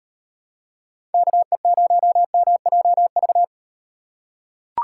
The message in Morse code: KE0MJV

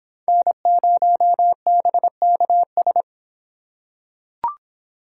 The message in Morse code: N0BKS

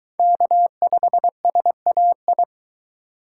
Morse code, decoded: K5SAI

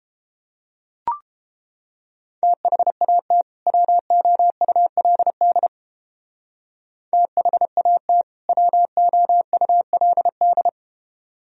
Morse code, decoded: THAT WOULD THAT WOULD

